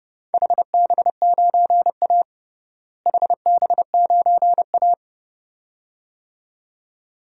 Morse code, heard as a keyed tone, about 32 wpm, 700 Hz. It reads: HB9A HB9A
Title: DXCC-100 Call Signs - 4 Characters - Repeated 2x - 30wpm